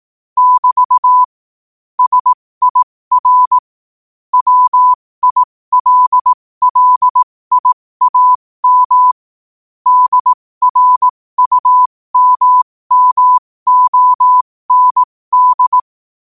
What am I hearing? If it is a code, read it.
Morse code, heard as a keyed tone, about 18 wpm, spaced as for 15 wpm, 1000 Hz. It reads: <BT> SIR WILLIAM DRUMMOND
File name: 15wpm